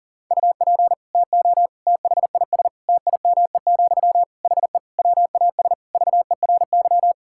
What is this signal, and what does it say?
Morse code, heard as a keyed tone, about 40 wpm, 700 Hz. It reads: UP TO THIS TIME, HE WAS VERY
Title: Sentences from Top 1000 Words - Review Part 6 - 40wpm